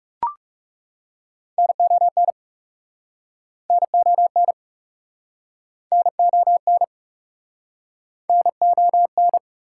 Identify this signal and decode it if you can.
Morse code, heard as a keyed tone, about 44 wpm, 700 Hz. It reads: NON NON NON NON